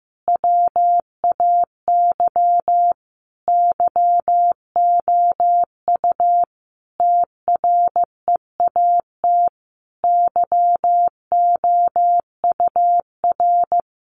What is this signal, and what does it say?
Morse code, heard as a keyed tone, about 14 wpm, 700 Hz. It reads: WAY YOU TREAT YOUR
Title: Sentences from Top 1300 Words 15wpm